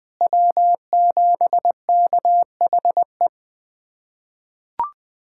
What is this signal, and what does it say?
Morse code, heard as a keyed tone, about 20 wpm, 700 Hz. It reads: W7KHE
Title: USA Callsign DE Callsign - 5 Characters - 20wpm